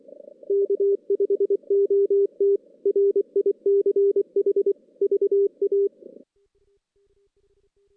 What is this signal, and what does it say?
Morse code, heard as a keyed tone, about 24 wpm, 400 Hz. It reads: K5OT RICH VA
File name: 2016 JAN CW NAQP / 2016 Jan naqp k5ot - 0-5